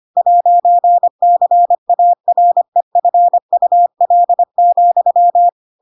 Morse code, heard as a keyed tone, about 25 wpm, 700 Hz. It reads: 'CAREFUL,
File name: Pinocchio / Pinocchio ch7 25 25 ik1hge